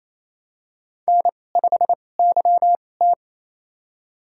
Morse code, heard as a keyed tone, about 27 wpm, spaced as for 20 wpm, 700 Hz. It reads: N5YT
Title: USA Call Signs - 2x Character Spacing - 28wpm